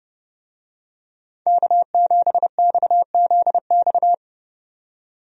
Morse code, heard as a keyed tone, about 30 wpm, 700 Hz. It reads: K7XZX
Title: USA Call Signs - 5 Characters - 30wpm